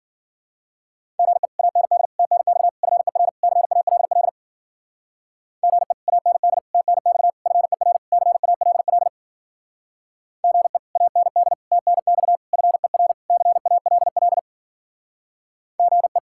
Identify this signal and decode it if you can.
Morse code, heard as a keyed tone, about 52 wpm, 700 Hz. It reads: GE AND TNX FER CALL GE AND TNX FER CALL GE AND TNX FER CALL GE